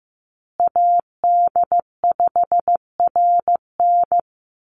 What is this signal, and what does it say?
Morse code, heard as a keyed tone, about 15 wpm, 700 Hz. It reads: AD5RN
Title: USA Call Signs Repeated 2x 15wpm